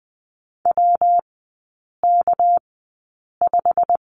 Morse code, heard as a keyed tone, about 20 wpm, spaced as for 8 wpm, 700 Hz. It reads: WK5
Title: 198 Lesson 32e - Letter k - Three Characters 20wpm